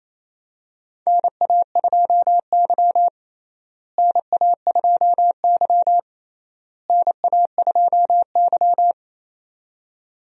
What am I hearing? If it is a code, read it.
Morse code, heard as a keyed tone, about 28 wpm, 700 Hz. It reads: NA2Y NA2Y NA2Y
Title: USA Call Signs - 4 Characters - Repeated 3x - 28wpm